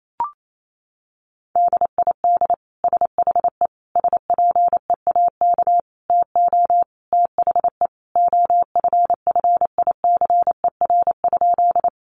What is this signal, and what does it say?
Morse code, heard as a keyed tone, about 28 wpm, 700 Hz. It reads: DID SHE SPEAK TO THE OFFICER?
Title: Sentences from Top 1200 Words 28wpm